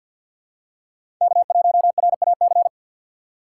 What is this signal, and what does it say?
Morse code, heard as a keyed tone, about 52 wpm, 700 Hz. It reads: K1RAC
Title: USA Call Signs - 6x Character Spacing - 50wpm